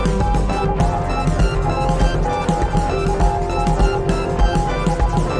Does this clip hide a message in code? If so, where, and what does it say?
Yes: Morse code from 0.2 to 5.2 s, heard as a keyed tone, about 20 wpm, 820 Hz. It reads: THE ROAD AS